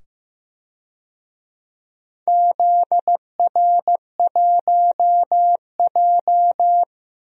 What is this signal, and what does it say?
Morse code, heard as a keyed tone, about 15 wpm, 700 Hz. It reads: ZR1J
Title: DXCC-100 Call Signs - 4 Characters - 15wpm